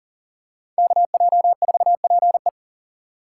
Morse code, heard as a keyed tone, about 40 wpm, 700 Hz. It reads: KJ4PE
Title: USA Call Signs - 5 Characters - 6x Character Spacing - 40wpm